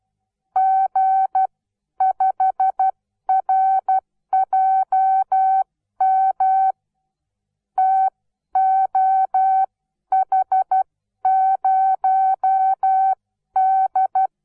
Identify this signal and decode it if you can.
Morse code, heard as a keyed tone, about 12 wpm, spaced as for 10 wpm, 760 Hz. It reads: G5RJM TOH0D
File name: Onlinelehrgang / morsen / DJ4UF Morsen17